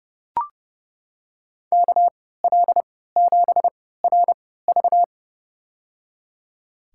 Morse code, heard as a keyed tone, about 32 wpm, spaced as for 17 wpm, 700 Hz. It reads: KL7RV
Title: USA Call Signs - 3x Character Spacing - 30wpm